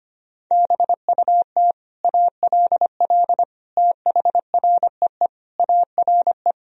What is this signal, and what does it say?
Morse code, heard as a keyed tone, about 25 wpm, 700 Hz. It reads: BUT ALL THREE ARE
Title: Sentences from Top 1000 Words - Review Part 3 - 25wpm